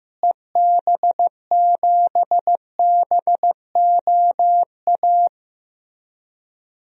Morse code, heard as a keyed tone, about 15 wpm, 700 Hz. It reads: EB7BOA
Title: DXCC-100 Call Signs - 6 Characters - Repeated 3x - 15wpm